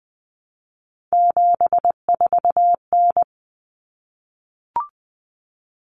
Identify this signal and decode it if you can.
Morse code, heard as a keyed tone, about 20 wpm, 700 Hz. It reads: 74N
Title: Common Cut Numbers - 3-Digit 20wpm